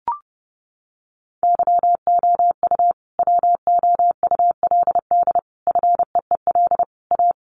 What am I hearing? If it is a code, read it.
Morse code, heard as a keyed tone, about 30 wpm, 700 Hz. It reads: YOU WOULD FEEL A
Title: Sentences from Top 600 Words 30wpm